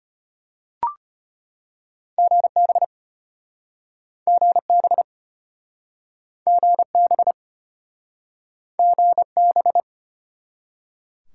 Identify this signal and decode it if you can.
Morse code, heard as a keyed tone, about 33 wpm, 700 Hz. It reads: GB GB GB GB